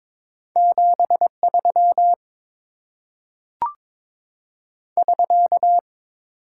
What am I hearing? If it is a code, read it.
Morse code, heard as a keyed tone, about 22 wpm, 700 Hz. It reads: 73 <SK>
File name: Top 90 QSO Elements - 8x Character Spacing - 22wpm